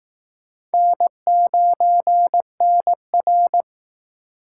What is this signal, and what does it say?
Morse code, heard as a keyed tone, about 19 wpm, 700 Hz. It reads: N9NR